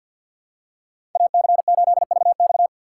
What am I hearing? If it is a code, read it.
Morse code, heard as a keyed tone, about 52 wpm, 700 Hz. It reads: AC8VX
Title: USA Call Signs - 5 Characters - 6x Character Spacing - 50wpm